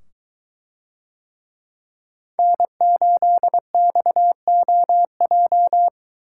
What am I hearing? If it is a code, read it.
Morse code, heard as a keyed tone, about 22 wpm, 700 Hz. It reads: N8XOJ